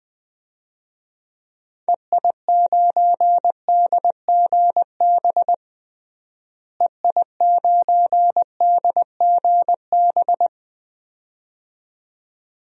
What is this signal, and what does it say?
Morse code, heard as a keyed tone, about 20 wpm, 700 Hz. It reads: EI9DGB EI9DGB
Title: DXCC-100 Call Signs - 6 Characters - Repeated 2x - 20wpm